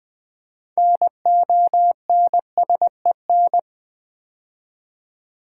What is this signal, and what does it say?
Morse code, heard as a keyed tone, about 20 wpm, 690 Hz. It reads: NONSEN
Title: Build A Word - 2 Letters At A Time - 8-Letter Words - 20wpm